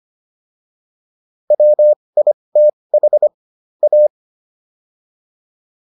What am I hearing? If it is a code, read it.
Morse code, heard as a keyed tone, about 25 wpm, spaced as for 20 wpm, 600 Hz. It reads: WITH A